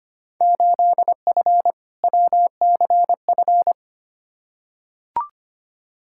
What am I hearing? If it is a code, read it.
Morse code, heard as a keyed tone, about 25 wpm, 700 Hz. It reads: 8F WCF
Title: Field Day Exchanges Encore - 25wpm